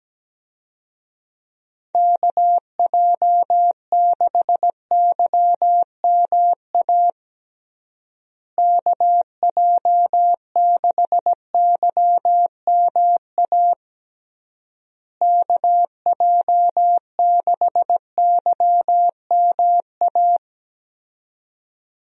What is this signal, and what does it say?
Morse code, heard as a keyed tone, about 17 wpm, 700 Hz. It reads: KJ6YMA KJ6YMA KJ6YMA
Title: USA Call Signs - 6 Characters - Repeated 3x - 17wpm